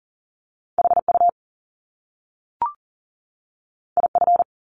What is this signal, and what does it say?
Morse code, heard as a keyed tone, about 38 wpm, 700 Hz. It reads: HU IF
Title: Top Two Letter Combinations in Words 40wpm